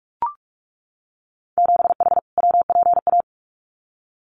Morse code, heard as a keyed tone, about 44 wpm, 700 Hz. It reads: 7H WPA
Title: Winter Field Day Exchanges Encore 45wpm